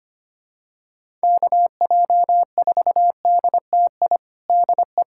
Morse code, heard as a keyed tone, about 25 wpm, 700 Hz. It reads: KJ4DTI DE
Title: USA Callsign DE Callsign 25wpm